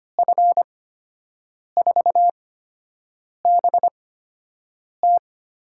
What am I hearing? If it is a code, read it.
Morse code, heard as a keyed tone, about 25 wpm, spaced as for 7 wpm, 700 Hz. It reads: F4BT
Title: USA Call Signs - 8x Character Spacing - 25wpm